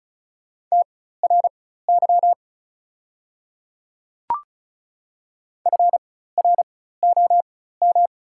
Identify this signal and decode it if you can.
Morse code, heard as a keyed tone, about 36 wpm, spaced as for 16 wpm, 690 Hz. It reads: TRY FROM
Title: 4x Character Spacing - Top 300 Words - 35wpm